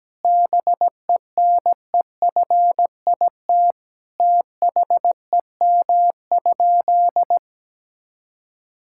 Morse code, heard as a keyed tone, about 17 wpm, 700 Hz. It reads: BENEFIT THEM?
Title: Sentences from Top 1000 Words - Review Part 6 - 17wpm